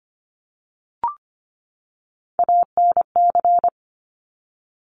Morse code, heard as a keyed tone, about 25 wpm, 700 Hz. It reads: ANC
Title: Top Three Letter Combinations in Words 25wpm